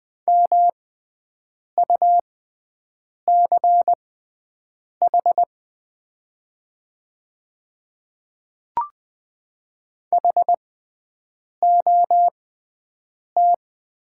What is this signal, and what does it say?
Morse code, heard as a keyed tone, about 20 wpm, spaced as for 7 wpm, 700 Hz. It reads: MUCH HOT
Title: 6x Character Spacing - Top 200 Words - 20wpm